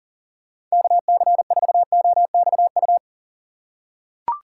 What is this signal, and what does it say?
Morse code, heard as a keyed tone, about 40 wpm, 700 Hz. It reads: KC4OXU
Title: USA Call Signs - 6 Characters - 8x Character Spacing - 40wpm